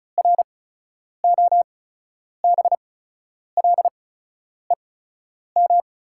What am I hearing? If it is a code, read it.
Morse code, heard as a keyed tone, about 36 wpm, spaced as for 10 wpm, 690 Hz. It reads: ROBLEM
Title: 8x Character Spacing - Top 500 Words - 35wpm